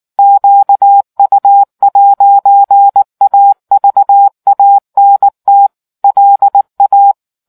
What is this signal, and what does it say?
Morse code, heard as a keyed tone, about 19 wpm, 800 Hz. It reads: QU'AVANT LA